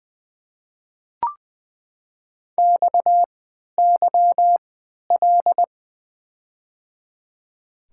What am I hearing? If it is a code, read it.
Morse code, heard as a keyed tone, about 20 wpm, spaced as for 11 wpm, 700 Hz. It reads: XYL
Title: Top 90 QSO Elements - 3x Character Spacing - 20wpm